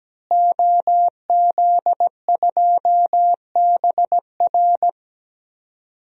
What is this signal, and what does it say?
Morse code, heard as a keyed tone, about 17 wpm, 700 Hz. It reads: OZ2BR
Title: DXCC-100 Call Signs - 5 Characters - Repeated 3x - 17wpm